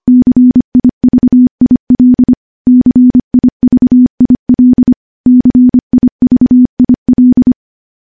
Morse code, heard as a keyed tone, about 25 wpm, 250 Hz. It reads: CIVIL CIVIL CIVIL
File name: morse / morse-training-25WPM